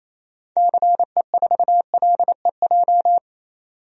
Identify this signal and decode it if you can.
Morse code, heard as a keyed tone, about 28 wpm, 700 Hz. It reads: CE4LEJ
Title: DXCC-100 Call Signs - 6 Characters - 28wpm